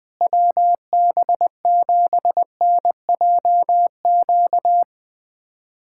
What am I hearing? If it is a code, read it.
Morse code, heard as a keyed tone, about 20 wpm, 700 Hz. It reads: WB7NJQ
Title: USA Callsign DE Callsign - 6 Characters - 20wpm